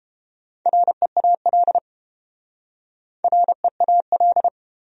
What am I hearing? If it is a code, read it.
Morse code, heard as a keyed tone, about 31 wpm, 700 Hz. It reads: REAL REAL